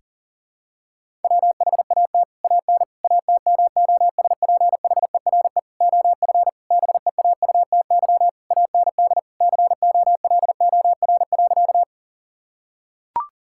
Morse code, heard as a keyed tone, about 40 wpm, 700 Hz. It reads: WHAT AN ATMOSPHERE OF BEAUTY AND COLOR.
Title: Sentences from Top 1600 Words 40wpm